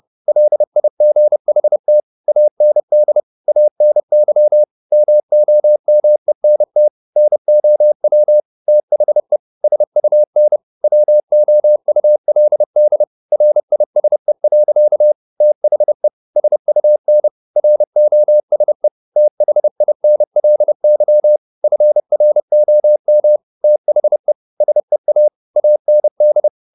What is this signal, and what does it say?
Morse code, heard as a keyed tone, about 30 wpm, 600 Hz. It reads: LIGHT AND ANY MOMENT NOW THE SUN WOULD RISE. THE SUN ROSE THINLY FROM THE SEA AND
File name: the old man and the sea / the old man and the sea 1 30wpm